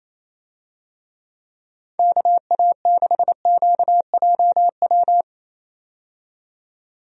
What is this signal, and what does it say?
Morse code, heard as a keyed tone, about 27 wpm, 700 Hz. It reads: KA6QJW
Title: USA Call Signs - 6 Characters - 28wpm